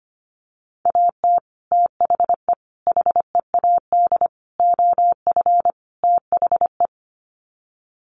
Morse code, heard as a keyed tone, about 25 wpm, 700 Hz. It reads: AT THE HEAD OF THE
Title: Sets of 5 Words 25wpm